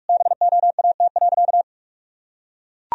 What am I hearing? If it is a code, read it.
Morse code, heard as a keyed tone, about 41 wpm, 700 Hz. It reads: BOAT.
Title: Sentences from Top 1000 Words - Review Part 3 - 45wpm